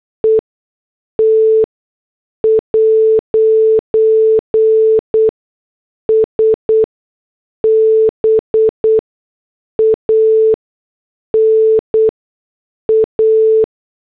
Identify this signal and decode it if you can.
Morse code, heard as a keyed tone, about 8 wpm, spaced as for 6 wpm, 440 Hz. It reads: ET'S BAN A